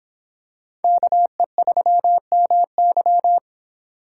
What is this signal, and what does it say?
Morse code, heard as a keyed tone, about 26 wpm, 700 Hz. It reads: KE3MY